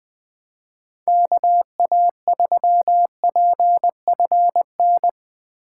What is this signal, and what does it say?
Morse code, heard as a keyed tone, about 21 wpm, 700 Hz. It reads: KA3PFN
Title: USA Call Signs - 6 Characters - 3x Character Spacing - 20wpm